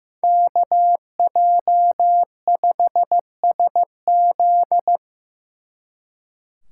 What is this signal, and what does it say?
Morse code, heard as a keyed tone, about 18 wpm, 700 Hz. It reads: KJ5SZ
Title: Speed Racing - USA Call Signs 15wpm